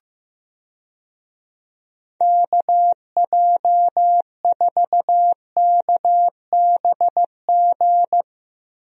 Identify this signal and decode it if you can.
Morse code, heard as a keyed tone, about 15 wpm, 700 Hz. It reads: KJ4KBG
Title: USA Call Signs - 6 Characters - Repeated 2x - 15wpm